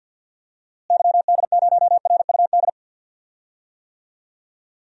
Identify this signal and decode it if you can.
Morse code, heard as a keyed tone, about 46 wpm, 700 Hz. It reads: YD0RUD
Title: DXCC-100 Call Signs - 6 Characters - Repeated 2x - 50wpm